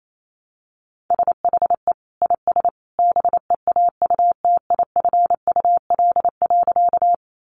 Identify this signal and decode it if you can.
Morse code, heard as a keyed tone, about 28 wpm, 700 Hz. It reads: SHE IS BEAUTIFUL.
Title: Sentences from Top 900 Words 28wpm